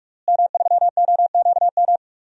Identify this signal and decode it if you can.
Morse code, heard as a keyed tone, about 46 wpm, 700 Hz. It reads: M3OQM
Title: DXCC-100 Call Signs - 5 Characters - Repeated 3x - 45wpm